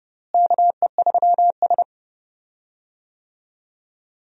Morse code, heard as a keyed tone, about 30 wpm, 700 Hz. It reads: KE3S
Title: USA Call Signs - 4 Characters - Repeated 2x - 30wpm